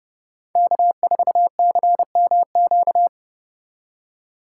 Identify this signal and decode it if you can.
Morse code, heard as a keyed tone, about 30 wpm, 700 Hz. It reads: K4CMQ
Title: USA Call Signs - 5 Characters - 3x Character Spacing - 30wpm